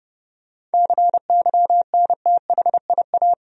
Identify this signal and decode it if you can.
Morse code, heard as a keyed tone, about 32 wpm, 700 Hz. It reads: CYNTHIA
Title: USA Common Names - 30wpm